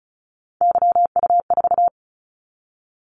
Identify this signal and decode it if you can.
Morse code, heard as a keyed tone, about 36 wpm, 700 Hz. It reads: YU4
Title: Callsign Prefixes for Top 25 DXCC - 35wpm